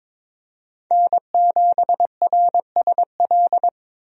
Morse code, heard as a keyed tone, about 22 wpm, 700 Hz. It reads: N7RSL
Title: USA Call Signs - 5 Characters - Repeated 3x - 22wpm